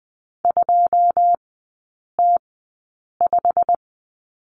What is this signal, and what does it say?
Morse code, heard as a keyed tone, about 20 wpm, spaced as for 8 wpm, 700 Hz. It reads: 2T5
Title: 107 Lesson 18e - Letter m - Three Characters 20wpm